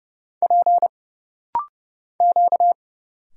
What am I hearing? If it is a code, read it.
Morse code, heard as a keyed tone, about 30 wpm, 700 Hz. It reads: P Q
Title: Single Letters - Rapid-Fire 30wpm